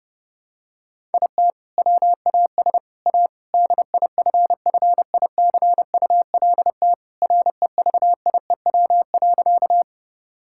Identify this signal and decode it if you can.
Morse code, heard as a keyed tone, about 32 wpm, 700 Hz. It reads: IT WAS A DIFFICULT REVIEW.
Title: Sentences from Top 1000 Words - Review Part 6 - 30wpm